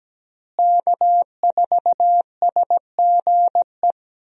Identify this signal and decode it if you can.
Morse code, heard as a keyed tone, about 17 wpm, 700 Hz. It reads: K4SGE
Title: USA Call Signs - 5 Characters - Repeated 3x - 17wpm